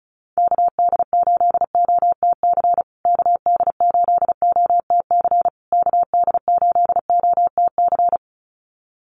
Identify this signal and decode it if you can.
Morse code, heard as a keyed tone, about 34 wpm, 700 Hz. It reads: KD8OTC KD8OTC KD8OTC
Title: USA Call Signs Repeated 3x 35wpm